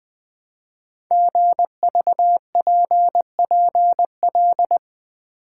Morse code, heard as a keyed tone, about 20 wpm, 700 Hz. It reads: GVPPL
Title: Code Groups - 5 Letters - 20wpm